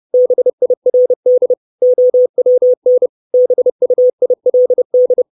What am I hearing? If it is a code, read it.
Morse code, heard as a keyed tone, about 30 wpm, 500 Hz. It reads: BIRD OWN BUILD
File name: 632 Words 30WPM